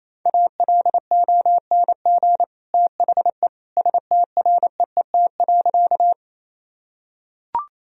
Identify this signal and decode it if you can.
Morse code, heard as a keyed tone, about 28 wpm, 700 Hz. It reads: ALONG THE STREET.
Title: Sentences from Top 1000 Words - Review Part 3 - 28wpm